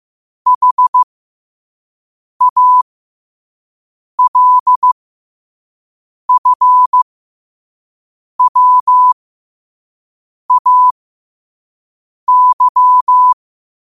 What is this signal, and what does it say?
Morse code, heard as a keyed tone, about 15 wpm, spaced as for 5 wpm, 1000 Hz. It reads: HALFWAY
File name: 5wpm